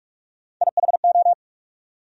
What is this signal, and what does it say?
Morse code, heard as a keyed tone, about 46 wpm, 700 Hz. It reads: IHO
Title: USA Call Sign Postfixes - 3 Letters 45wpm